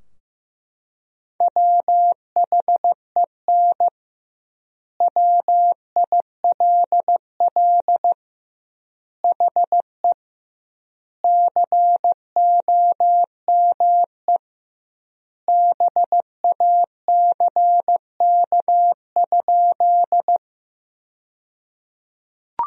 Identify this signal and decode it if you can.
Morse code, heard as a keyed tone, about 16 wpm, 700 Hz. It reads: WHEN WILL HE COME BACK?
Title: 2x Word-Spacing - Sentences from Top 200 Words - 15wpm